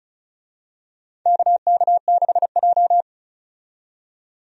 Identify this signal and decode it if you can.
Morse code, heard as a keyed tone, about 36 wpm, 700 Hz. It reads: KK6J
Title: USA Call Signs - 4 Characters - 35wpm